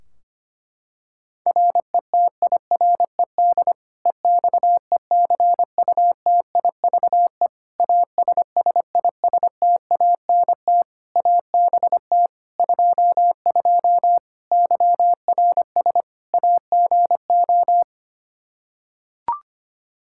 Morse code, heard as a keyed tone, about 25 wpm, 700 Hz. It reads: RETIRED EXECUTIVE ASSISTANT ABT 22 YRS AGO
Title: Practice QSO - Exchange 3 - More Info - 25wpm